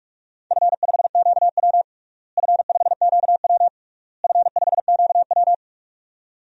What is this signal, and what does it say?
Morse code, heard as a keyed tone, about 43 wpm, 690 Hz. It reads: F5QW F5QW F5QW
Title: DXCC-100 Call Signs - 4 Characters - Repeated 3x - 45wpm